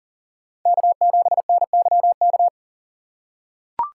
Morse code, heard as a keyed tone, about 40 wpm, 700 Hz. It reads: K7NYK
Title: USA Call Signs - 3x Character Spacing - 40wpm